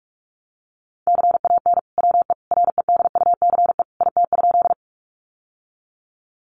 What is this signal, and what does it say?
Morse code, heard as a keyed tone, about 44 wpm, 700 Hz. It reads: CAN WE REDUCE IT?
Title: Sentences from Top 1100 Words 45wpm